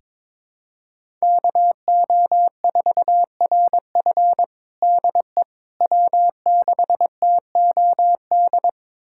Morse code, heard as a keyed tone, about 22 wpm, 700 Hz. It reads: KO4RF DE W6TOD
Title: USA Callsign DE Callsign - 5 Characters - 22wpm